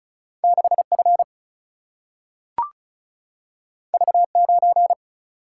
Morse code, heard as a keyed tone, about 36 wpm, 700 Hz. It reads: 6 F V 9
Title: Single-Letter Single-Number 35wpm